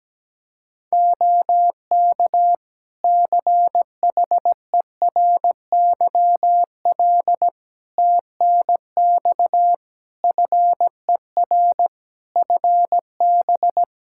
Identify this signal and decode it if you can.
Morse code, heard as a keyed tone, about 17 wpm, 700 Hz. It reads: OK CHERYL TNX FER FB
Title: Practice QSO - Exchange 4 - Ending - 17wpm